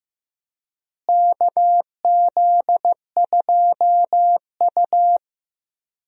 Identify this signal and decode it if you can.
Morse code, heard as a keyed tone, about 15 wpm, 700 Hz. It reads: KZ2U
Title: USA Call Signs - 4 Characters - Repeated 2x - 15wpm